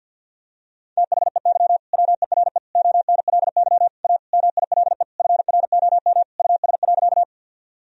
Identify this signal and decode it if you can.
Morse code, heard as a keyed tone, about 52 wpm, 700 Hz. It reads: THEY WERE ONLY A MILE FROM US.
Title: Sentences from Top 1000 Words - Review Part 1 - 50wpm